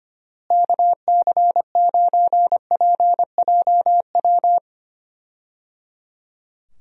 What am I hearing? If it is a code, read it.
Morse code, heard as a keyed tone, about 25 wpm, 700 Hz. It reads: KC9PJW
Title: USA Call Signs - 6 Characters - Repeated 3x - 25wpm